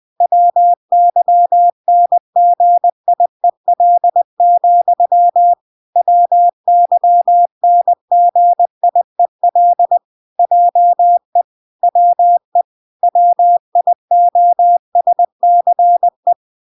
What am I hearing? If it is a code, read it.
Morse code, heard as a keyed tone, about 20 wpm, 700 Hz. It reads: WYNGIEL, WYNGIEL JE WE WIOSCE